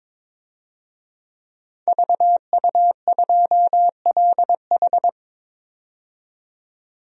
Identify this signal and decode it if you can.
Morse code, heard as a keyed tone, about 22 wpm, 700 Hz. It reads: VU2LH
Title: DXCC-100 Call Signs - 5 Characters - 22wpm